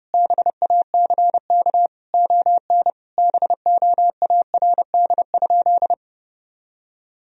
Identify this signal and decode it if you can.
Morse code, heard as a keyed tone, about 30 wpm, 700 Hz. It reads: BACK ON BOARD?
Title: Sentences from Top 1000 Words - Review Part 4 - 30wpm